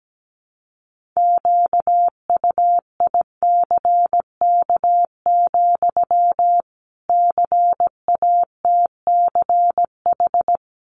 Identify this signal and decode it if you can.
Morse code, heard as a keyed tone, about 17 wpm, 700 Hz. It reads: QUICK, CATCH
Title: Sentences from Top 1500 Words 17wpm